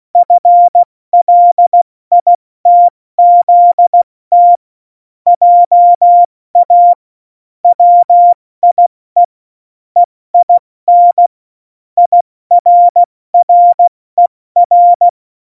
Morse code, heard as a keyed tone, about 16 wpm, spaced as for 14 wpm, 700 Hz. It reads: FLITZT JA WIE EIN IRRER
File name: Morsekurs / Witze / Witz 07 April 2020, 70 F80